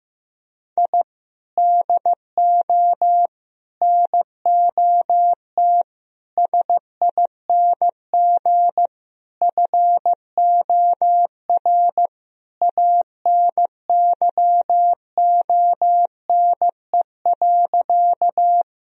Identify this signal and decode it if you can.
Morse code, heard as a keyed tone, about 15 wpm, 700 Hz. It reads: I DO NOT SING FOR ANYONE.
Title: Sentences from Top 1000 Words - Review Part 2 - 15wpm